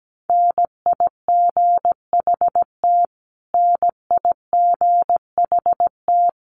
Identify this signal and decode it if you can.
Morse code, heard as a keyed tone, about 17 wpm, 700 Hz. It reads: NIGHT NIGHT
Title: Top 300 Words Repeated 17wpm